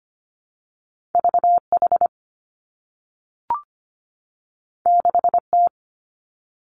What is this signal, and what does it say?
Morse code, heard as a keyed tone, about 25 wpm, 700 Hz. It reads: V H 6 T
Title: Two Challenging Letter-Number Combinations 25wpm